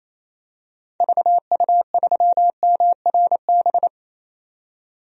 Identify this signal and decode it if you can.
Morse code, heard as a keyed tone, about 28 wpm, 700 Hz. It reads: VU3MRB
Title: DXCC-100 Call Signs - 6 Characters - 28wpm